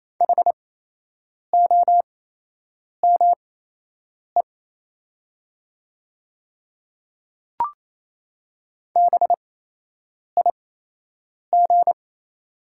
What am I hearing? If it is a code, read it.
Morse code, heard as a keyed tone, about 28 wpm, spaced as for 8 wpm, 700 Hz. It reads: HOME BIG
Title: 8x Character Spacing - Top 200 Words - 28wpm